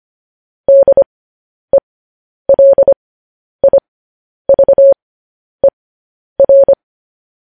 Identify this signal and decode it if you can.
Morse code, heard as a keyed tone, about 25 wpm, spaced as for 10 wpm, 550 Hz. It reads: DELIVER
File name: podcast / News headlines-1x-25x10